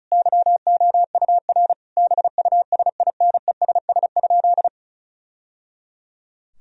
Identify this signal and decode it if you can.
Morse code, heard as a keyed tone, about 34 wpm, 700 Hz. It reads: YOUR BUSINESS?
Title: Sentences from Top 1000 Words - Review Part 4 - 35wpm